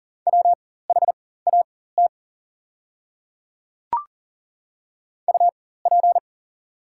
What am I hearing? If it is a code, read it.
Morse code, heard as a keyed tone, about 40 wpm, spaced as for 19 wpm, 690 Hz. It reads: WHAT UP
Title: 4x Character Spacing - Top 100 Words - 40wpm